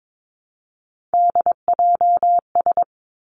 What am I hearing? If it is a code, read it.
Morse code, heard as a keyed tone, about 22 wpm, 700 Hz. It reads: DJS
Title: Three Challenging Letter-Number Combinations 22wpm